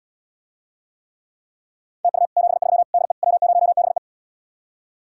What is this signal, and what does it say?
Morse code, heard as a keyed tone, about 79 wpm, 700 Hz. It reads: TU 73 DE W0BE